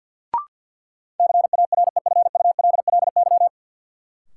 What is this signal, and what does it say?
Morse code, heard as a keyed tone, about 53 wpm, 700 Hz. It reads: CAREFULLY
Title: Sentences from Top 1300 Words - New Word Drill - 50wpm